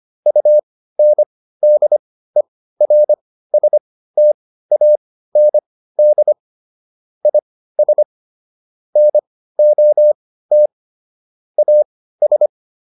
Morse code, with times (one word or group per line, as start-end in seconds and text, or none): 0.3-6.3 UNDERSTAND
7.3-8.0 IS
9.0-10.7 NOT
11.6-12.5 AS